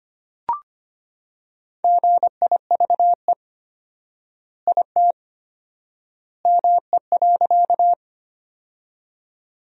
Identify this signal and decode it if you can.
Morse code, heard as a keyed tone, about 25 wpm, 700 Hz. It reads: GIVE IT ME.
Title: 4x Word-Spacing - Sentences from Top 200 Words - 25wpm